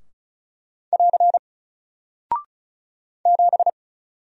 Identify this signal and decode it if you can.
Morse code, heard as a keyed tone, about 36 wpm, 700 Hz. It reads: <AR> 7
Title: Single Character (Letter:Number:Punctuation:Prosign) Alt 35wpm